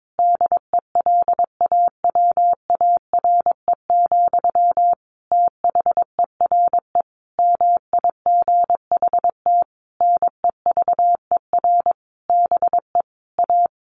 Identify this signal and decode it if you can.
Morse code, heard as a keyed tone, about 22 wpm, 700 Hz. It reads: DELAWARE, THERE MIGHT NEVER BE A
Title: A Sound Of Thunder 22wpm